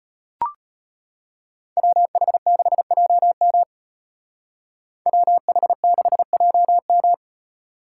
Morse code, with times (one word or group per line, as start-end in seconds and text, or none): 1.8-3.6 WH6JM
5.1-7.1 WH6JM